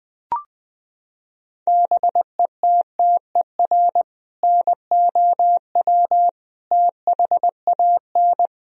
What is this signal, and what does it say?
Morse code, heard as a keyed tone, about 21 wpm, 700 Hz. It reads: BETTER NOW THAN
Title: Sentences from Top 1000 Words - Review Part 4 - 20wpm